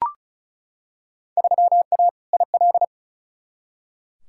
Morse code, heard as a keyed tone, about 34 wpm, 700 Hz. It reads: 3A IL
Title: Field Day Exchanges - 35wpm